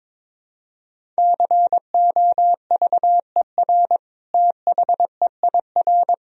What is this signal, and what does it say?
Morse code, heard as a keyed tone, about 21 wpm, 700 Hz. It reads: COVER THEIR
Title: Sentences from Top 1000 Words - Review Part 1 - 22wpm